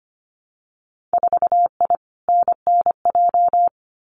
Morse code, heard as a keyed tone, about 25 wpm, 700 Hz. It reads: 4I NNJ
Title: Winter Field Day Exchanges 25wpm